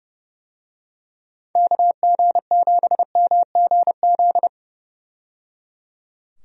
Morse code, heard as a keyed tone, about 30 wpm, 700 Hz. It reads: KG7MGZ
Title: USA Call Signs - 6 Characters - 30wpm